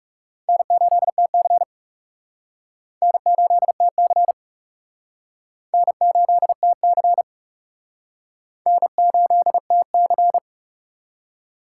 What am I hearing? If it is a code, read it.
Morse code, heard as a keyed tone, about 38 wpm, 700 Hz. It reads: N8TC N8TC N8TC N8TC